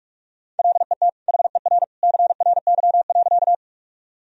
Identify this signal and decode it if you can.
Morse code, heard as a keyed tone, about 46 wpm, 700 Hz. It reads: LET HER CRY.
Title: Sentences from Top 1000 Words - Review Part 2 - 45wpm